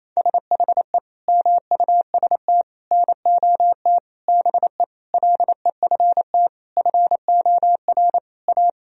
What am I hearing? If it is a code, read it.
Morse code, heard as a keyed tone, about 28 wpm, 700 Hz. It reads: SHE MUST NOT BE LEFT FOR A
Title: Sentences from Top 1000 Words - Review Part 1 - 28wpm